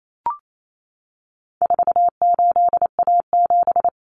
Morse code, heard as a keyed tone, about 28 wpm, 700 Hz. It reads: 48A7
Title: Common Cut Numbers - 4-Digit 28wpm